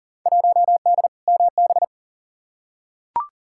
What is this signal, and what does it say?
Morse code, heard as a keyed tone, about 40 wpm, 700 Hz. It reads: 1D MB
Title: Field Day Exchanges Encore - 40wpm